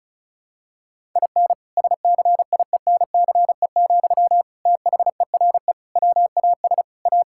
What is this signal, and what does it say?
Morse code, heard as a keyed tone, about 36 wpm, 700 Hz. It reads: IN SCIENCE, THERE WAS A
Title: Sentences from Top 1000 Words - Review Part 5 - 35wpm